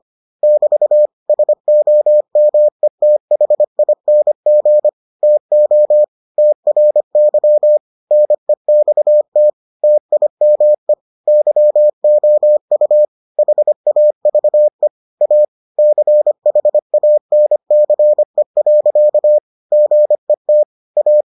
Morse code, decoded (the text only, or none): <BT> SOMETHING TO TRY NEXT TIME YOU HAVE A CHANCE. GET A